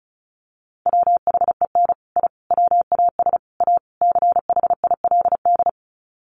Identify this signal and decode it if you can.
Morse code, heard as a keyed tone, about 34 wpm, 700 Hz. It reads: WHEN I WAS A CHILD
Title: Sets of 5 Words Encore 35wpm